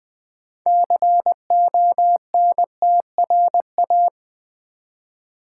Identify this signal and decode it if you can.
Morse code, heard as a keyed tone, about 20 wpm, 700 Hz. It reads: CONTRA
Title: Build A Word - 3 Letters At A Time - 8-Letter Words - 20wpm